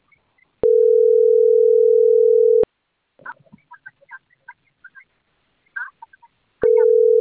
Ringback tone (North America), 440 + 480 Hz.